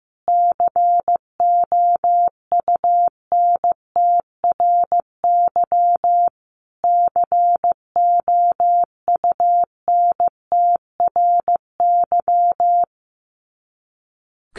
Morse code, heard as a keyed tone, about 15 wpm, 700 Hz. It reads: COUNTRY COUNTRY
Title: Top 500 Words Repeated 15wpm